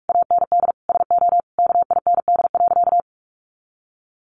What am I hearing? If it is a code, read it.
Morse code, heard as a keyed tone, about 44 wpm, 700 Hz. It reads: AND SO KIND.
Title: Sentences from Top 1500 Words 45wpm